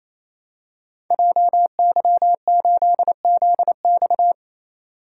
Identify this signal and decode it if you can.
Morse code, heard as a keyed tone, about 27 wpm, 700 Hz. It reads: JY8ZX
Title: Code Groups - 5 Letter:Numbers - 28wpm